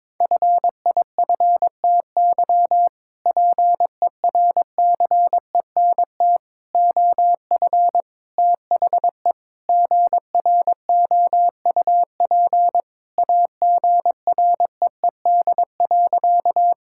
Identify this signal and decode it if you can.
Morse code, heard as a keyed tone, about 22 wpm, 700 Hz. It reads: FIFTY PERCENT OF THE GROUP AGREED.
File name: practice / 22wpm / Sentences with 3-Word Phrases - 22wpm